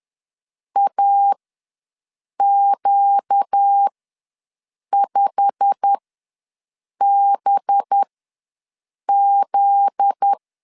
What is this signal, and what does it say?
Morse code, heard as a keyed tone, about 10 wpm, spaced as for 6 wpm, 800 Hz. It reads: AQ5BZ
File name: LEZIONI / 9 28 ws 10wpm 4124